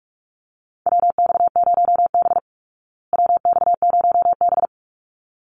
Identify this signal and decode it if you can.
Morse code, heard as a keyed tone, about 44 wpm, 700 Hz. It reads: WX0B WX0B
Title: Call Signs Repeated 3x - 45wpm